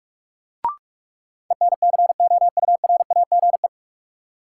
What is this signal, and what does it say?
Morse code, heard as a keyed tone, about 46 wpm, 700 Hz. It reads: ENCOURAGE
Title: Sentences from Top 1400 Words - New Word Drill - 45wpm